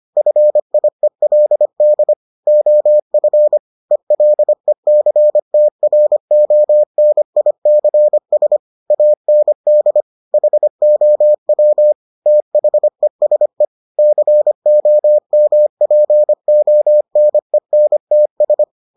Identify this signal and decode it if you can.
Morse code, heard as a keyed tone, about 25 wpm, 600 Hz. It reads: FIELD OF ELECTRONICS AND HOW THESE COMPONENTS